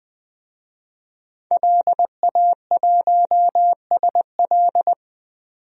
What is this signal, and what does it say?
Morse code, heard as a keyed tone, about 21 wpm, 700 Hz. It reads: LA1SL
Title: DXCC-100 Call Signs - 5 Characters - 20wpm